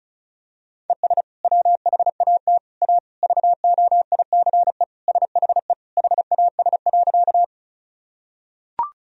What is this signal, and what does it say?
Morse code, heard as a keyed tone, about 36 wpm, 700 Hz. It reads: ES WHAT A VOICE SHE HAS.
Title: Sentences from Top 500 Words and Abbreviations 35wpm